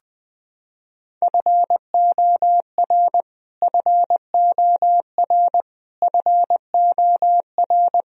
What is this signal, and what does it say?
Morse code, heard as a keyed tone, about 20 wpm, 700 Hz. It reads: FOR FOR FOR
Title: Top 100 Words Repeated 20wpm